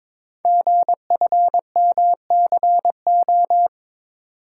Morse code, heard as a keyed tone, about 22 wpm, 700 Hz. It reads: GFMCO
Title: Code Groups - 5 Letters - 22wpm